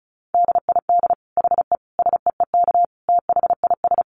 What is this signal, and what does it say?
Morse code, heard as a keyed tone, about 34 wpm, 700 Hz. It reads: DID HE SEEK THIS
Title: Dit Challenge 35wpm